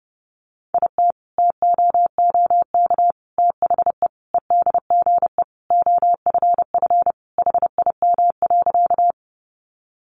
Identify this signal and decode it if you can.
Morse code, heard as a keyed tone, about 30 wpm, 700 Hz. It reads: IT TOOK THE EDGE OFF HIM.
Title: Sentences from Top 1300 Words 30wpm